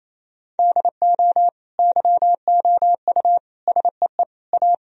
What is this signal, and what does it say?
Morse code, heard as a keyed tone, about 28 wpm, 700 Hz. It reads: DO YOU SEE A
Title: Sentences from Top 1000 Words - Review Part 4 - 28wpm